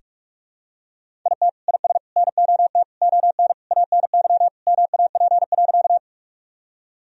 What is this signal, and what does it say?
Morse code, heard as a keyed tone, about 46 wpm, 700 Hz. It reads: IT IS NOT ON ANY MAP.
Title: Sentences from Top 1000 Words - Review Part 2 - 45wpm